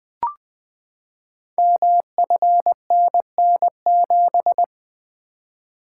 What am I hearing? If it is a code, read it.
Morse code, heard as a keyed tone, about 20 wpm, 700 Hz. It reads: MFNN7
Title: Code Groups - 5 Letter:Numbers - 20wpm